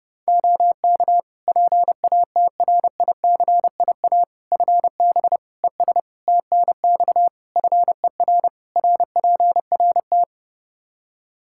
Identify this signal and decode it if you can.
Morse code, heard as a keyed tone, about 30 wpm, 700 Hz. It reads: OK PATRICIA FB ES TNX FER RPRT
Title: Practice QSO - Exchange 2 - Info - 30wpm